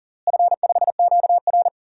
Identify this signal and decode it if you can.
Morse code, heard as a keyed tone, about 38 wpm, 700 Hz. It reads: F5QR
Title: DXCC-100 Call Signs - 4 Characters - Repeated 2x - 40wpm